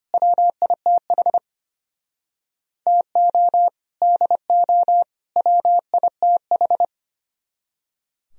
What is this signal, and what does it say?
Morse code, heard as a keyed tone, about 33 wpm, 700 Hz. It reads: WITH TO DO WITH